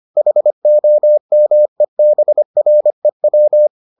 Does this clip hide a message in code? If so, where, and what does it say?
Yes: Morse code from 0.2 to 3.7 s, heard as a keyed tone, about 25 wpm, 600 Hz. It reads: HOMEBREW